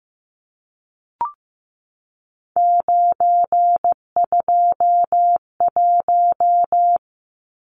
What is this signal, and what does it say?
Morse code, heard as a keyed tone, about 15 wpm, 700 Hz. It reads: 921
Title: Numbers - 3-Digit 15wpm